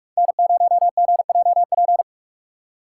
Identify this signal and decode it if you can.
Morse code, heard as a keyed tone, about 46 wpm, 700 Hz. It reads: N0GJP
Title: USA Call Signs - 2x Character Spacing - 45wpm